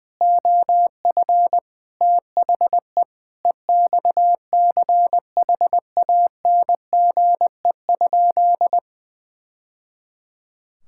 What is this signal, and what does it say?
Morse code, heard as a keyed tone, about 20 wpm, 700 Hz. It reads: OF THE EXCHANGE?
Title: Sentences from Top 1000 Words - Review Part 6 - 20wpm